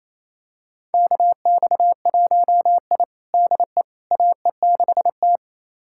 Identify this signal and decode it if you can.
Morse code, heard as a keyed tone, about 28 wpm, 700 Hz. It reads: KX1I DE AE6T
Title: USA Callsign DE Callsign - 4 Characters - 28wpm